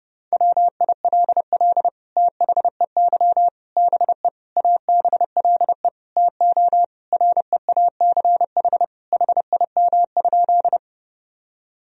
Morse code, heard as a keyed tone, about 30 wpm, 700 Hz. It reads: WILL THEY BE ABLE TO REACH HIM?
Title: Sentences from Top 1000 Words - Review Part 2 - 30wpm